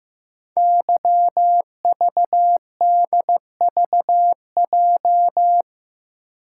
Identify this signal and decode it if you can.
Morse code, heard as a keyed tone, about 15 wpm, 700 Hz. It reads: YVDVJ
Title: Code Groups - 5 Letters - 15wpm